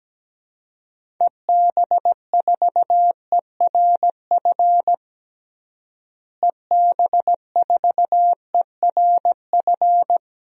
Morse code, heard as a keyed tone, about 17 wpm, 700 Hz. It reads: EB4ERF EB4ERF
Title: DXCC-100 Call Signs - 6 Characters - Repeated 2x - 17wpm